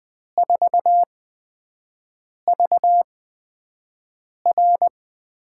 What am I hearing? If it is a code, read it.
Morse code, heard as a keyed tone, about 20 wpm, spaced as for 5 wpm, 700 Hz. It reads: 4VR
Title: USA Call Signs - 5 Characters - 8x Character Spacing - 20wpm